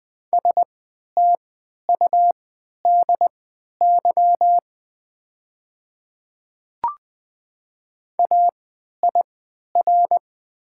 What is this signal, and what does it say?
Morse code, heard as a keyed tone, about 20 wpm, spaced as for 11 wpm, 700 Hz. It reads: STUDY AIR
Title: 3x Character Spacing - Top 300 Words - 20wpm